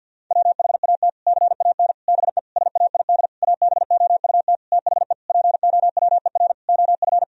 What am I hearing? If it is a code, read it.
Morse code, heard as a keyed tone, about 50 wpm, 700 Hz. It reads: WHAT CAN BE SAID ABOUT THE POWER OF
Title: Sentences from Top 1000 Words - Review Part 4 - 50wpm